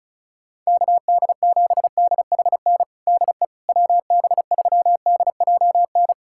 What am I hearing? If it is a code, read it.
Morse code, heard as a keyed tone, about 36 wpm, 700 Hz. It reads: KD7DHN DE WB3DJN
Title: USA Callsign DE Callsign 35wpm